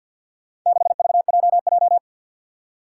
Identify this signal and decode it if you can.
Morse code, heard as a keyed tone, about 53 wpm, 700 Hz. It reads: 6VJJ
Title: Four Challenging Letter-Number Combinations 50wpm